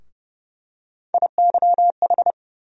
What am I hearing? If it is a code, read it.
Morse code, heard as a keyed tone, about 30 wpm, 700 Hz. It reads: IYH
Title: USA Call Sign Postfixes - 3 Letters 30wpm